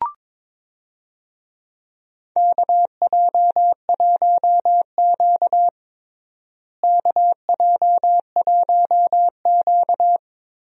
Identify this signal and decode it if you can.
Morse code, heard as a keyed tone, about 22 wpm, 700 Hz. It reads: KJ1Q KJ1Q
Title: USA Call Signs - 4 Characters - Repeated 2x - 22wpm